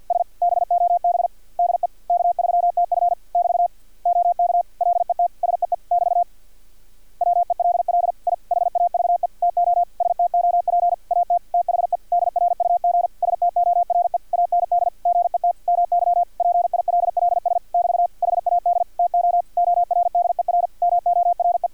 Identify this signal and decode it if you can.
Morse code, heard as a keyed tone, about 50 wpm, 700 Hz, with radio noise. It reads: S BOB DE K3TF <BT> OK LET SEE <BT> WELL I HAVE TO STOP AT THE DRUG STORE AND GET MY PILLS <BT> HAD TO ORDER MORE
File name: archived / CRSnet